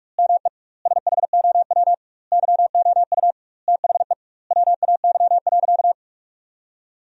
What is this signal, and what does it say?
Morse code, heard as a keyed tone, about 42 wpm, 700 Hz. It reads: ME SHOW YOU THE WAY.
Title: 2x Word-Spacing - Sentences from Top 300 Words - 45wpm